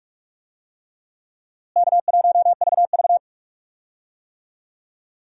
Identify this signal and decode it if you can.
Morse code, heard as a keyed tone, about 46 wpm, 700 Hz. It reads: K1VV
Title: USA Call Signs - 4 Characters - Repeated 3x - 45wpm